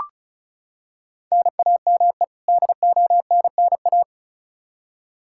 Morse code, heard as a keyed tone, about 34 wpm, 700 Hz. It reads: NAME DONNA
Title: Practice QSO - Exchange 1 - Beginning - 35wpm